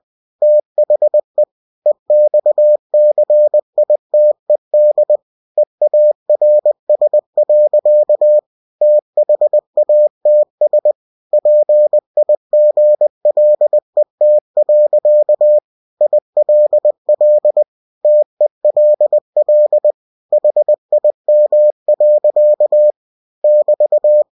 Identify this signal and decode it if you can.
Morse code, heard as a keyed tone, about 20 wpm, 600 Hz. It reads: THE EXCITED EARS. THATS PIGLET. ILL TELL HIM. <BT>
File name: cw-books / winnie-the-pooh / winnie-20wpm-page-0105